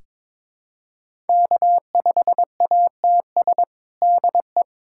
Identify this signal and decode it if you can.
Morse code, heard as a keyed tone, about 22 wpm, 700 Hz. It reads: K5ATS DE
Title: USA Callsign DE Callsign - 5 Characters - 22wpm